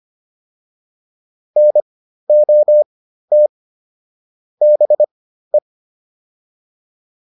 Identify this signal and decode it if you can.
Morse code, heard as a keyed tone, about 25 wpm, spaced as for 13 wpm, 600 Hz. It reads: NOT BE